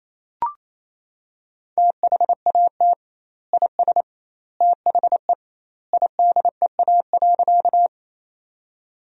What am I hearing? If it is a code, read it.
Morse code, heard as a keyed tone, about 28 wpm, 700 Hz. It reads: THAT IS THE IDEA.
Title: 2x Word-Spacing - Sentences from Top 300 Words - 28wpm